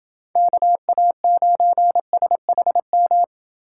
Morse code, heard as a keyed tone, about 27 wpm, 700 Hz. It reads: KA9SHM